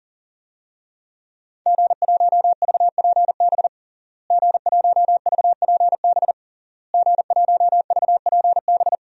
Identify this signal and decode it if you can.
Morse code, heard as a keyed tone, about 40 wpm, 700 Hz. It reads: G1VPB G1VPB G1VPB
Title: DXCC-100 Call Signs - 5 Characters - Repeated 3x - 40wpm